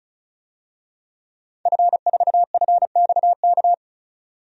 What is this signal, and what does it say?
Morse code, heard as a keyed tone, about 36 wpm, 700 Hz. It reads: F4FXK
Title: DXCC-100 Call Signs - 5 Characters - 35wpm